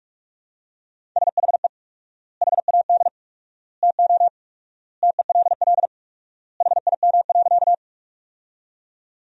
Morse code, heard as a keyed tone, about 45 wpm, 700 Hz. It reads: SHE HAD TO TELL HIM.
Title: 4x Word-Spacing - Sentences from Top 200 Words - 45wpm